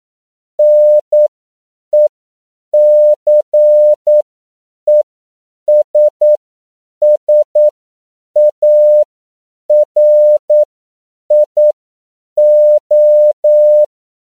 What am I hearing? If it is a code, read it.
Morse code, heard as a keyed tone, about 9 wpm, spaced as for 7 wpm, 600 Hz. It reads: NECESSARIO
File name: CW / 10PPM